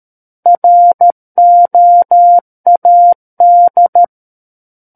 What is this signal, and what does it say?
Morse code, heard as a keyed tone, about 13 wpm, 700 Hz. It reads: ROAD
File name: Book One / 04-Chapter 4 - 13 WPM